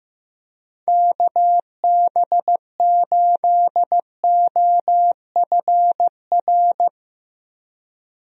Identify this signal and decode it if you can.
Morse code, heard as a keyed tone, about 15 wpm, 700 Hz. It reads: KB8OFR
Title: USA Call Signs - 6 Characters - 15wpm